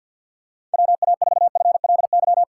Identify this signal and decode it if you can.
Morse code, heard as a keyed tone, about 46 wpm, 700 Hz. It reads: WA4FLY
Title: USA Call Signs - 6 Characters - 3x Character Spacing - 50wpm